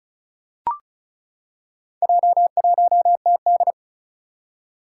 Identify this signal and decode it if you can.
Morse code, heard as a keyed tone, about 36 wpm, 700 Hz. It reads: J1TD
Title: Four Challenging Letter-Number Combinations 35wpm